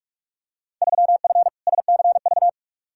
Morse code, heard as a keyed tone, about 46 wpm, 700 Hz. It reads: 3F SCV
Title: Field Day Exchanges - 45wpm